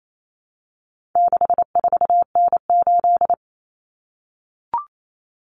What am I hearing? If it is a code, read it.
Morse code, heard as a keyed tone, about 28 wpm, 700 Hz. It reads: 64N8
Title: Common Cut Numbers - 4-Digit 28wpm